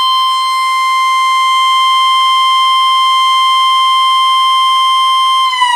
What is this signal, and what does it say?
Emergency Alert System (SAME) tones (the 1050 Hz attention tone), with noise. It is heard from 0.3 to 5.5 s.
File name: VXS-8200